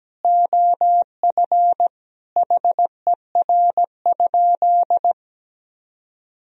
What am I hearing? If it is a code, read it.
Morse code, heard as a keyed tone, about 17 wpm, 700 Hz. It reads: OF HER?
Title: Sentences from Top 200 Words and Abbreviations 17wpm